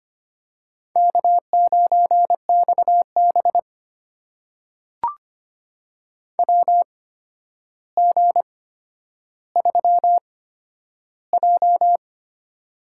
Morse code, heard as a keyed tone, about 25 wpm, spaced as for 11 wpm, 700 Hz. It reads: K9XB WG3J
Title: USA Call Signs - 4 Characters - 8x Character Spacing - 25wpm